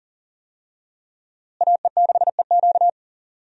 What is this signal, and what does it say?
Morse code, heard as a keyed tone, about 40 wpm, 700 Hz. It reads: AE6EQ
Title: USA Call Signs - 5 Characters - Repeated 3x - 40wpm